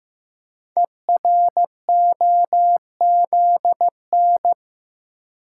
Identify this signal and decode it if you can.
Morse code, heard as a keyed tone, about 15 wpm, 700 Hz. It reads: EROZN
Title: Code Groups - 5 Letters - 15wpm